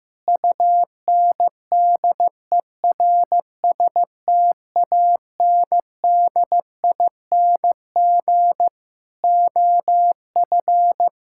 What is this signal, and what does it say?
Morse code, heard as a keyed tone, about 15 wpm, 700 Hz. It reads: UNDERSTANDING OF
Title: Sentences from Top 1000 Words - Review Part 6 - 15wpm